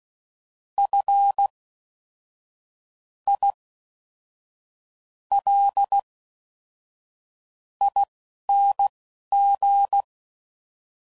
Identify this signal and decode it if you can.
Morse code, heard as a keyed tone, about 16 wpm, spaced as for 6 wpm, 800 Hz. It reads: FILING